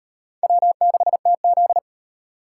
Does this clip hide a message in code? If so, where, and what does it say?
Yes: Morse code from 0.4 to 1.8 s, heard as a keyed tone, about 38 wpm, 700 Hz. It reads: W6TZ